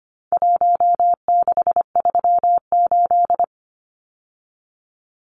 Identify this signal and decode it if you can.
Morse code, heard as a keyed tone, about 25 wpm, 700 Hz. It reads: 1638
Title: Numbers - Contest Style - 4-Digit 25wpm